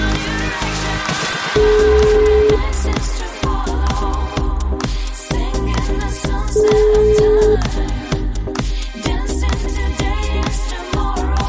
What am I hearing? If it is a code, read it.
Ringback tone (Europe, 425 Hz), with noise.